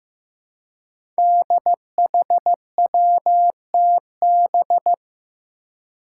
Morse code, heard as a keyed tone, about 15 wpm, 700 Hz. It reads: DHWTB
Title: Five Challenging Letter-Number Combinations 15wpm